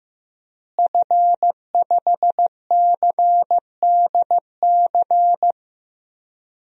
Morse code, heard as a keyed tone, about 15 wpm, 700 Hz. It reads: F5CDC
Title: DXCC-100 Call Signs - 5 Characters - Repeated 2x - 15wpm